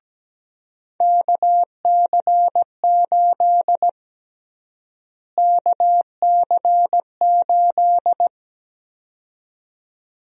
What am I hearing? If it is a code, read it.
Morse code, heard as a keyed tone, about 17 wpm, 700 Hz. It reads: KC8 KC8
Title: USA Call Sign Prefixes - Repeated 2x - 17wpm